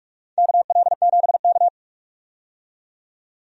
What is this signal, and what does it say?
Morse code, heard as a keyed tone, about 47 wpm, 700 Hz. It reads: KL7K
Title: DXCC-100 Call Signs - 4 Characters - 45wpm